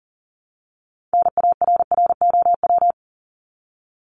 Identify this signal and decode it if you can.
Morse code, heard as a keyed tone, about 38 wpm, 700 Hz. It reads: NARROW
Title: Top 1501-1750 Words 40wpm